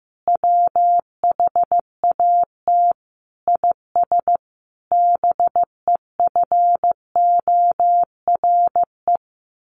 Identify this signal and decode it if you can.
Morse code, heard as a keyed tone, about 15 wpm, 700 Hz. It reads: WHAT IS BEFORE
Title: Sentences from Top 600 Words 15wpm